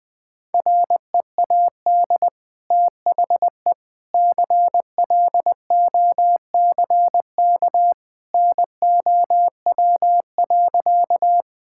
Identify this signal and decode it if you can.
Morse code, heard as a keyed tone, about 19 wpm, 700 Hz. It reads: READ THE CLOCK NOW.
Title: Easy Sentences with Hints - 20wpm